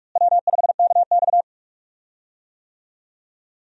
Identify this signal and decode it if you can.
Morse code, heard as a keyed tone, about 47 wpm, 700 Hz. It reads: W5KX
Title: USA Call Signs - 4 Characters - Repeated 3x - 45wpm